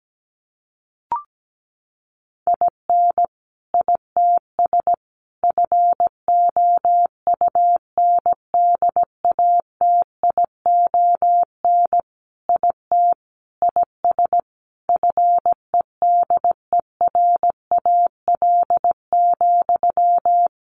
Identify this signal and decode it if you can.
Morse code, heard as a keyed tone, about 17 wpm, 700 Hz. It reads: IN ITS FOUNDATION IT IS FEDERAL,
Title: Sentences from Top 1200 Words 17wpm